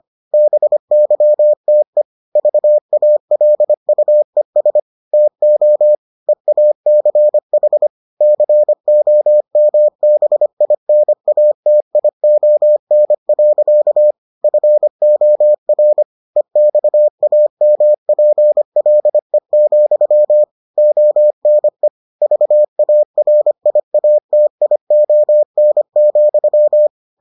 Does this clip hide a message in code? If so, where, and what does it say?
Yes: Morse code from 0.3 to 26.9 s, heard as a keyed tone, about 25 wpm, 600 Hz. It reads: BYTE VALUES TO EACH COMBINATION. FOR EXAMPLE, ONE VARIATION,